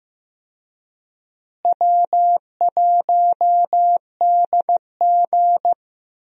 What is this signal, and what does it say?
Morse code, heard as a keyed tone, about 15 wpm, 700 Hz. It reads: W1DG
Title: USA Call Signs - 4 Characters - Repeated 3x - 15wpm